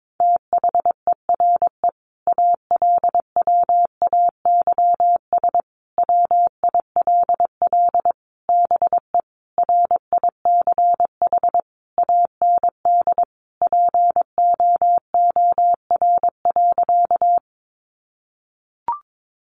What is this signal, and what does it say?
Morse code, heard as a keyed tone, about 22 wpm, 700 Hz. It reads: THERE ALWAYS WILL BE RICH AND POOR.
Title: Sentences from Top 1100 Words 22wpm